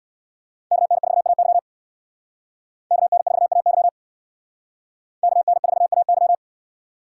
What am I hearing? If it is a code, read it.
Morse code, heard as a keyed tone, about 62 wpm, 690 Hz. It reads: KN4AY KN4AY KN4AY
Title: Speed Racing - USA 2x2 Call Signs 50wpm